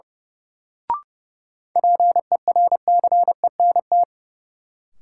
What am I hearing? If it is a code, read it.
Morse code, heard as a keyed tone, about 30 wpm, 700 Hz. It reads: PERCENT
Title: Sets of 2 Words - New Word Drill - 30wpm